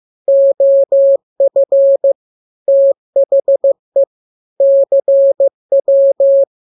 Morse code, heard as a keyed tone, about 15 wpm, 550 Hz. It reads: OF THE CW
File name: jokes / joke 3 15 0000